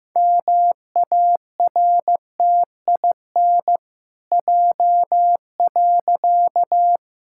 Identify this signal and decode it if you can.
Morse code, heard as a keyed tone, about 15 wpm, 700 Hz. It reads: MARTIN J.
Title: Pangram Challenge v2 - 15wpm